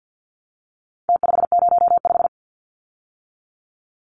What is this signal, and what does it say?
Morse code, heard as a keyed tone, about 50 wpm, 700 Hz. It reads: T505
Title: Numbers - Contest Style - 4-Digit 50wpm